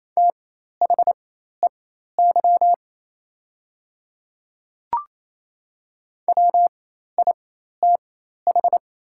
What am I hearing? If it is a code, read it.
Morse code, heard as a keyed tone, about 27 wpm, spaced as for 13 wpm, 700 Hz. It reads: THEY WITH
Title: 4x Character Spacing - Top 100 Words - 28wpm